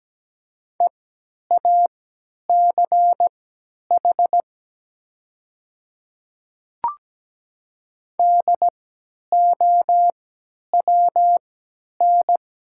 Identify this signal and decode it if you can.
Morse code, heard as a keyed tone, about 17 wpm, spaced as for 10 wpm, 700 Hz. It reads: EACH DOWN
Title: 3x Character Spacing - Top 100 Words - 17wpm